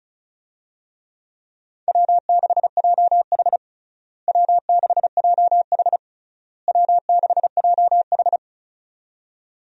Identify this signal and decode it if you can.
Morse code, heard as a keyed tone, about 36 wpm, 700 Hz. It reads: W6JH W6JH W6JH
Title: USA Call Signs - 4 Characters - Repeated 3x - 35wpm